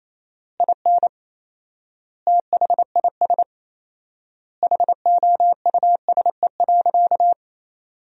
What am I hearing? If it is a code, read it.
Morse code, heard as a keyed tone, about 28 wpm, 700 Hz. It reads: IN THIS HOUSE.
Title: 4x Word-Spacing - Sentences from Top 400 Words - 28wpm